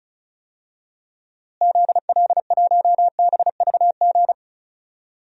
Morse code, heard as a keyed tone, about 36 wpm, 700 Hz. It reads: ZL1BVG
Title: DXCC-100 Call Signs - 6 Characters - Repeated 3x - 35wpm